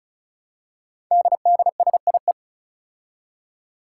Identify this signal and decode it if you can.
Morse code, heard as a keyed tone, about 35 wpm, 700 Hz. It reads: DDSIE
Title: Five Challenging Letter-Number Combinations 35wpm